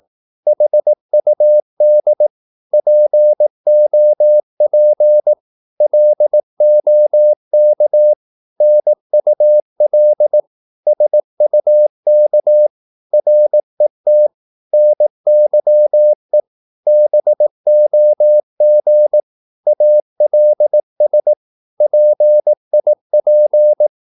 Morse code, heard as a keyed tone, about 18 wpm, 600 Hz. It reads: HUD POP LOK NUL SUK RET NYE BOG ALS PIP
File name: Korte ord DK 18wpm